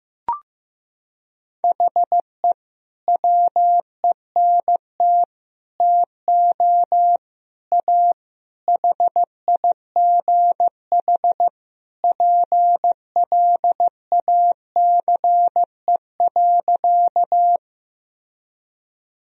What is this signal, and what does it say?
Morse code, heard as a keyed tone, about 15 wpm, 700 Hz. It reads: HE WENT TO A HIGH PLACE.
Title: Sentences from Top 1000 Words - Review Part 1 - 15wpm